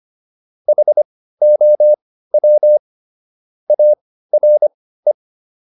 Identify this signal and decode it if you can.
Morse code, heard as a keyed tone, about 25 wpm, spaced as for 15 wpm, 600 Hz. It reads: HOW ARE